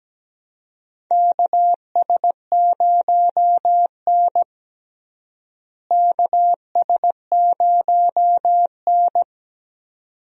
Morse code, heard as a keyed tone, about 17 wpm, 700 Hz. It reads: KS0N KS0N
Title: USA Call Signs - 4 Characters - Repeated 2x - 17wpm